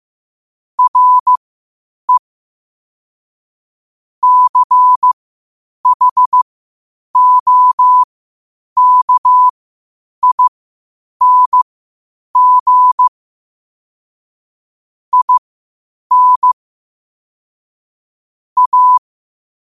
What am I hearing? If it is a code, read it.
Morse code, heard as a keyed tone, about 15 wpm, spaced as for 8 wpm, 1000 Hz. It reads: RE CHOKING IN A
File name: qotdfiles / 8wpm